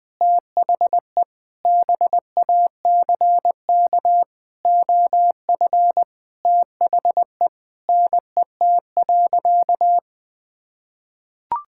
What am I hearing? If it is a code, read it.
Morse code, heard as a keyed tone, about 21 wpm, 700 Hz. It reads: THE BACK OF THE NET.
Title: Sentences with 9-Word Phrases - 20wpm